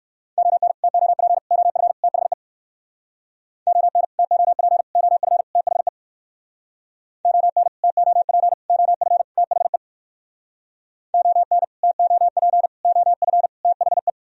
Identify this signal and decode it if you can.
Morse code, heard as a keyed tone, about 59 wpm, 700 Hz. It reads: ON TOP OF THE ON TOP OF THE ON TOP OF THE ON TOP OF THE